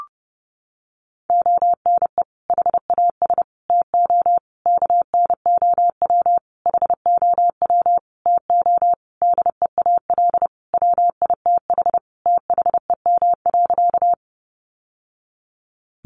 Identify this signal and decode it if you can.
Morse code, heard as a keyed tone, about 30 wpm, 700 Hz. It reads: ONE HAS TO KNOW HOW TO DEAL WITH THEM.
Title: Sentences from Top 800 Words 30wpm